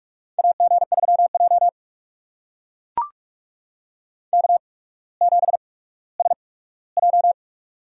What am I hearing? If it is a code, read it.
Morse code, heard as a keyed tone, about 44 wpm, spaced as for 19 wpm, 700 Hz. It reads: AG3J K7SJ
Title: USA Call Signs - 4 Characters - 8x Character Spacing - 45wpm